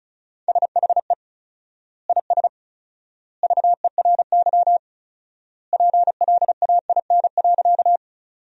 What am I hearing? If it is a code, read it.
Morse code, heard as a keyed tone, about 34 wpm, 700 Hz. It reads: SHE IS VERY PLAIN.
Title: 4x Word-Spacing - Sentences from Top 400 Words - 35wpm